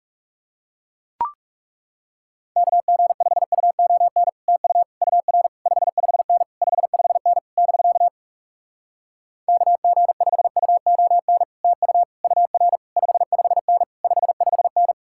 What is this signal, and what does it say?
Morse code, heard as a keyed tone, about 40 wpm, 700 Hz. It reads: KG5UON TU UR 55N 55N <BK> KG5UON TU UR 55N 55N